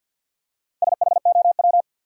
Morse code, heard as a keyed tone, about 53 wpm, 700 Hz. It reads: SHOW
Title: Four Letter Words - 50wpm